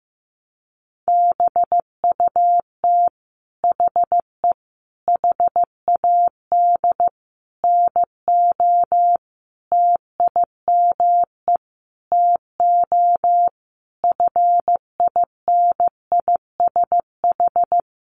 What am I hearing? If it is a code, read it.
Morse code, heard as a keyed tone, about 15 wpm, 700 Hz. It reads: BUT HE HAD NO TIME TO FINISH
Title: Sentences from Top 1200 Words 15wpm